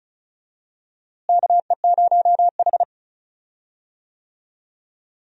Morse code, heard as a keyed tone, about 34 wpm, 700 Hz. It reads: KE0H
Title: USA Call Signs - 4 Characters - Repeated 2x - 35wpm